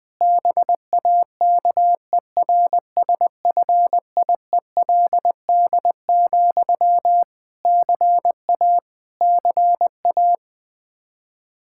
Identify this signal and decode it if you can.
Morse code, heard as a keyed tone, about 20 wpm, 700 Hz. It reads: BAKERSFIELD, CA CA
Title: Practice QSO - Exchange 1 - Beginning - 20wpm